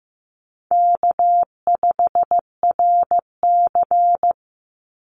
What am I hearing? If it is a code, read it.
Morse code, heard as a keyed tone, about 15 wpm, 700 Hz. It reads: K5RC
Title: Easy Call Signs Repeated 2x - 15wpm